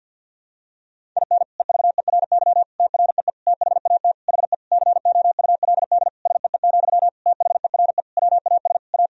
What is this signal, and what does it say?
Morse code, heard as a keyed tone, about 51 wpm, 700 Hz. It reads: IN EVERY TREE THAT HE COULD SEE, THERE WAS A
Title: Sentences from Top 1000 Words - Review Part 1 - 50wpm